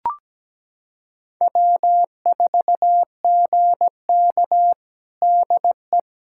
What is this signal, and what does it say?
Morse code, heard as a keyed tone, about 17 wpm, 700 Hz. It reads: W4GK DE
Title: USA Callsign DE Callsign - 4 Characters - 17wpm